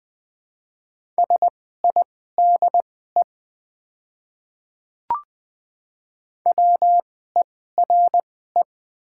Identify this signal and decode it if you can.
Morse code, heard as a keyed tone, about 20 wpm, spaced as for 15 wpm, 690 Hz. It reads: SIDE WERE
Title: 2x Character Spacing - Top 300 Words - 20wpm